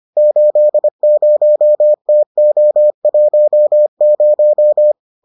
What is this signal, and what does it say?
Morse code, heard as a keyed tone, about 25 wpm, 600 Hz. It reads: 80TO10